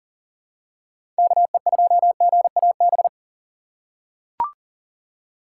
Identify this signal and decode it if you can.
Morse code, heard as a keyed tone, about 39 wpm, 700 Hz. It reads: KE2GAB
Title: USA Call Signs - 6 Characters - 3x Character Spacing - 40wpm